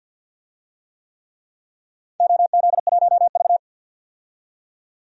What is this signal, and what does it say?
Morse code, heard as a keyed tone, about 49 wpm, 700 Hz. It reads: OZ1V
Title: DXCC-100 Call Signs - 4 Characters - 50wpm